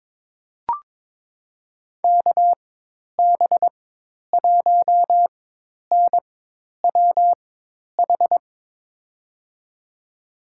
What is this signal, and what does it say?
Morse code, heard as a keyed tone, about 23 wpm, spaced as for 10 wpm, 700 Hz. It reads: KB1NWH
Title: USA Call Signs - 6 Characters - 4x Character Spacing - 22wpm